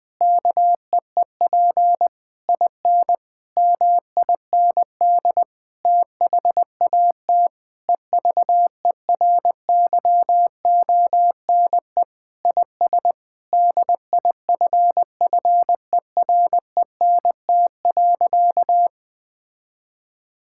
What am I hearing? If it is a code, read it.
Morse code, heard as a keyed tone, about 20 wpm, 700 Hz. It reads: KEEP IN MIND THAT EVERYONE IS DIFFERENT.
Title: Sentences with 8-Word Phrases - 20wpm